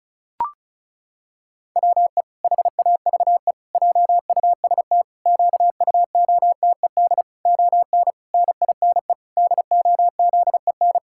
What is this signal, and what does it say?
Morse code, heard as a keyed tone, about 36 wpm, 700 Hz. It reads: WE HAVE JUST QUOTED ON NINE DOZEN
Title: Pangram Challenge v2 - 35wpm